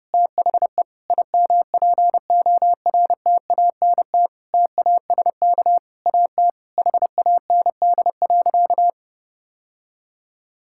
Morse code, heard as a keyed tone, about 30 wpm, 700 Hz. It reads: THE IMPORTANT TASK AT HAND.
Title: Sentences from Top 1600 Words 30wpm